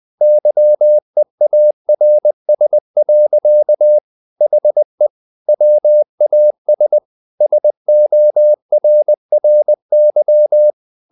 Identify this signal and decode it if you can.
Morse code, heard as a keyed tone, about 20 wpm, 600 Hz. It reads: YEARS. HE WAS SORRY
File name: the old man and the sea / the old man and the sea part 1 20wpm